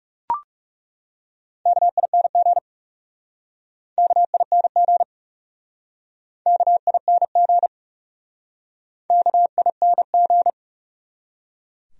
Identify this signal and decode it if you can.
Morse code, heard as a keyed tone, about 40 wpm, 700 Hz. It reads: KING KING KING KING